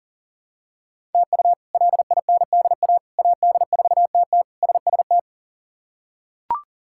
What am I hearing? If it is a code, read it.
Morse code, heard as a keyed tone, about 37 wpm, 700 Hz. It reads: TU LINDA AD4TT SST
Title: K1USN SST QSO Practice - 40wpm